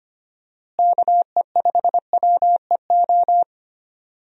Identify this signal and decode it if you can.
Morse code, heard as a keyed tone, about 25 wpm, 700 Hz. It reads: KE5WEO
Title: USA Call Signs - 6 Characters - Repeated 2x - 25wpm